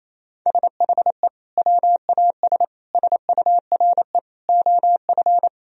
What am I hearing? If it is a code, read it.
Morse code, heard as a keyed tone, about 28 wpm, 700 Hz. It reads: SHE WAS SURE OF
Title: Sentences from Top 1000 Words - Review Part 1 - 28wpm